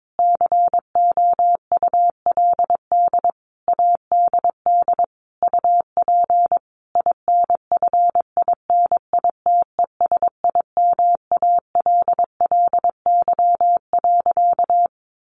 Morse code, heard as a keyed tone, about 22 wpm, 700 Hz. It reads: COULD ADD UP INFINITESIMALLY.
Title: A Sound Of Thunder 22wpm